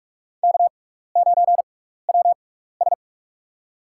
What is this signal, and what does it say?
Morse code, heard as a keyed tone, about 46 wpm, spaced as for 16 wpm, 700 Hz. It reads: K9WS
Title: USA Call Signs - 4 Characters - 6x Character Spacing - 45wpm